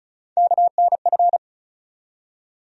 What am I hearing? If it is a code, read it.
Morse code, heard as a keyed tone, about 36 wpm, 700 Hz. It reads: KNF
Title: USA Call Sign Postfixes - 3 Letters 35wpm